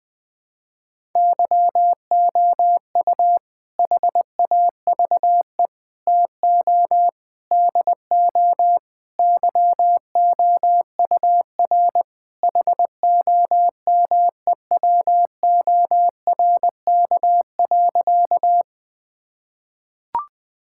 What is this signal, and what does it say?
Morse code, heard as a keyed tone, about 21 wpm, 700 Hz. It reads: YOU HAVE TO DO YOUR HOMEWORK.
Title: Sentences with 4-Word Phrases - 20wpm